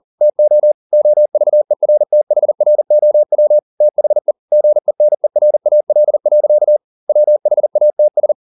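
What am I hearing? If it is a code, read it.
Morse code, heard as a keyed tone, about 40 wpm, 600 Hz. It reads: TO OVERTHROW THE GENERAL. WHATS
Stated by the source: Morse code at 40 wpm, practice text, CW med Gnister.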